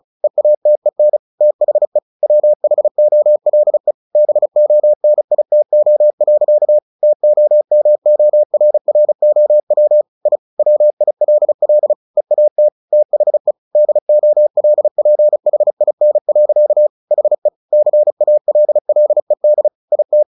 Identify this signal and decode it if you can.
Morse code, heard as a keyed tone, about 35 wpm, 600 Hz. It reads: EATEN THE WHOLE BONITO. TOMORROW I WILL EAT THE DOLPHIN. HE CALLED IT
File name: the old man and the sea 2 35wpm